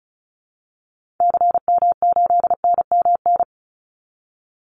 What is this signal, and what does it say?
Morse code, heard as a keyed tone, about 34 wpm, 700 Hz. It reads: CM8NMN
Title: Call Signs Repeated 3x - 35wpm